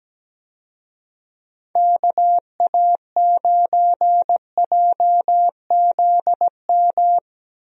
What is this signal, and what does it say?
Morse code, heard as a keyed tone, about 17 wpm, 700 Hz. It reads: KA9JZM
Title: USA Call Signs - 6 Characters - Repeated 2x - 17wpm